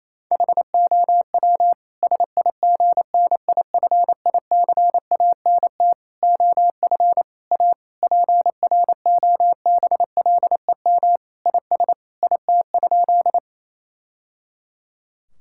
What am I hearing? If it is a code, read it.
Morse code, heard as a keyed tone, about 28 wpm, 700 Hz. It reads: HOW SIGNIFICANT OF A PROBLEM IS IT?
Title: Sentences from Top 1000 Words - Review Part 6 - 28wpm